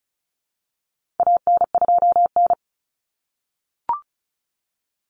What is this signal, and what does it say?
Morse code, heard as a keyed tone, about 34 wpm, 700 Hz. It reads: AN2N
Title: Common Cut Numbers - 4-Digit 35wpm